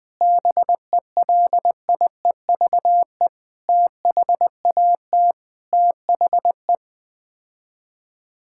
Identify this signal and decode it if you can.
Morse code, heard as a keyed tone, about 20 wpm, 700 Hz. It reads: BELIEVE THAT THE
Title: Sets of 8 Words - 20wpm